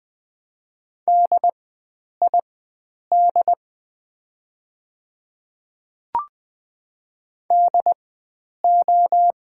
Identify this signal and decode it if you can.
Morse code, heard as a keyed tone, about 20 wpm, spaced as for 9 wpm, 700 Hz. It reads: DID DO
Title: 4x Character Spacing - Top 100 Words - 20wpm